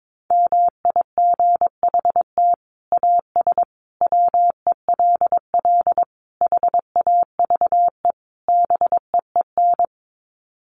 Morse code, heard as a keyed tone, about 22 wpm, 700 Hz. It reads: MIGHT AS WELL HAVE BEEN
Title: Sets of 5 Words Encore2 22wpm